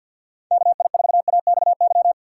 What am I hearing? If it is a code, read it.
Morse code, heard as a keyed tone, about 54 wpm, 700 Hz. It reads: KI4AXY
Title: USA Call Signs - 6 Characters - 8x Character Spacing - 50wpm